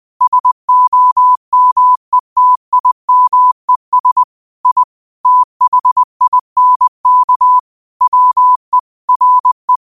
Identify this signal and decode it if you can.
Morse code, heard as a keyed tone, about 20 wpm, 1000 Hz. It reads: SOMETIMES I THINK WE RE
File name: qotdfiles / 20wpm